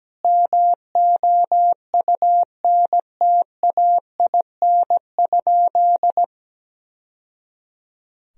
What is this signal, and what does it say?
Morse code, heard as a keyed tone, about 17 wpm, 700 Hz. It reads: MOUNTAIN?
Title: Sentences from Top 1000 Words - Review Part 1 - 17wpm